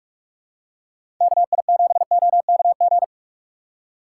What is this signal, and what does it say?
Morse code, heard as a keyed tone, about 46 wpm, 700 Hz. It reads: KI7OKG
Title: USA Call Signs - 6 Characters - 3x Character Spacing - 45wpm